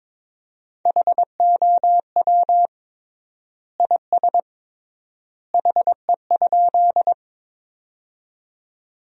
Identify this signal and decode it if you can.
Morse code, heard as a keyed tone, about 22 wpm, 700 Hz. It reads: HOW IS HE?
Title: 3x Word-Spacing - Sentences from Top 100 Words - 22wpm